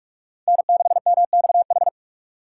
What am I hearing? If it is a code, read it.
Morse code, heard as a keyed tone, about 42 wpm, 700 Hz. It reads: N6MXH